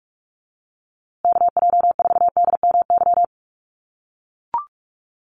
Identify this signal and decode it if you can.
Morse code, heard as a keyed tone, about 44 wpm, 700 Hz. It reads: KJ4DMY
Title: USA Call Signs Repeated 2x 45wpm